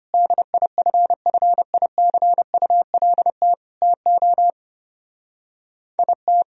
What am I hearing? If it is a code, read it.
Morse code, heard as a keyed tone, about 31 wpm, 700 Hz. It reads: DIFFICULT TO IT
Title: Speed-Racing - Sets of 4 Words 25wpm